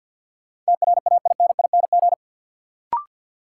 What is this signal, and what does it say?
Morse code, heard as a keyed tone, about 52 wpm, 700 Hz. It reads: TRAINING
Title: Sentences from Top 1000 Words - Review Part 4 - New Word Drill - 50wpm